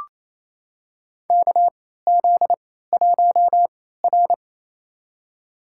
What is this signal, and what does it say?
Morse code, heard as a keyed tone, about 28 wpm, spaced as for 16 wpm, 700 Hz. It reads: KZ1R
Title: USA Call Signs - 4 Characters - 3x Character Spacing - 28wpm